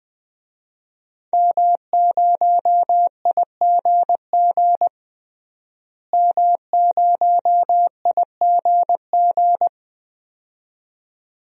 Morse code, heard as a keyed tone, about 20 wpm, 700 Hz. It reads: M0IGG M0IGG
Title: DXCC-100 Call Signs - 5 Characters - Repeated 2x - 20wpm